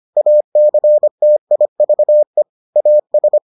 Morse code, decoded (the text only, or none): ACTIVE AS